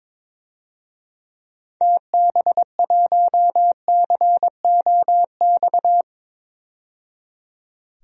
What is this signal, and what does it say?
Morse code, heard as a keyed tone, about 22 wpm, 700 Hz. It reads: TB1COX
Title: DXCC-100 Call Signs - 6 Characters - 22wpm